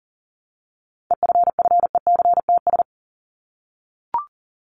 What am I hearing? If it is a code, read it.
Morse code, heard as a keyed tone, about 40 wpm, 700 Hz. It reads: EFFECTS
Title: ES:S Suffix - Top 100 Words with Base Words 40wpm